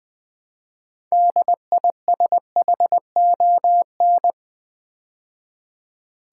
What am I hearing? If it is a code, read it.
Morse code, heard as a keyed tone, about 20 wpm, 700 Hz. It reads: DISHON
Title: Build A Word - 3 Letters At A Time - 9-Letter Words - 20wpm